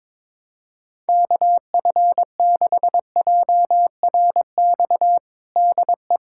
Morse code, heard as a keyed tone, about 22 wpm, 700 Hz. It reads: KF6JRX DE
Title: USA Callsign DE Callsign 22wpm